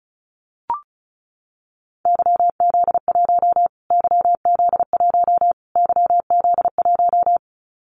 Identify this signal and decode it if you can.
Morse code, heard as a keyed tone, about 34 wpm, 700 Hz. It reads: YZ1 YZ1 YZ1
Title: Callsign Prefixes for Top 25 DXCC - Repeated 3x - 35wpm